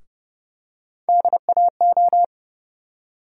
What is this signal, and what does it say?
Morse code, heard as a keyed tone, about 30 wpm, 700 Hz. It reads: DAO
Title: USA Call Sign Postfixes - 3 Letters 30wpm